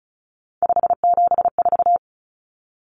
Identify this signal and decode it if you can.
Morse code, heard as a keyed tone, about 34 wpm, 700 Hz. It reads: 574
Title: Numbers - 3-Digit 35wpm